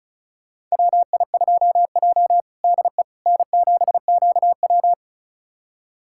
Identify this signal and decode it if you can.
Morse code, heard as a keyed tone, about 36 wpm, 700 Hz. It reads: WI2J DE N7QW
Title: USA Callsign DE Callsign - 4 Characters - 35wpm